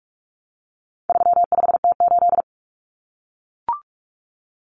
Morse code, heard as a keyed tone, about 44 wpm, 700 Hz. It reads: 35T8
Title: Common Cut Numbers - 4-Digit 45wpm